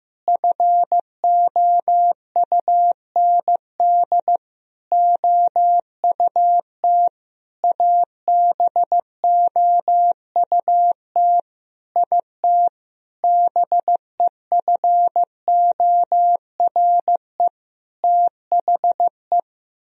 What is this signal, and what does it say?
Morse code, heard as a keyed tone, about 15 wpm, 700 Hz. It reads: FOUND OUT ABOUT IT BEFORE THE
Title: Sentences from Top 1000 Words - Review Part 3 - 15wpm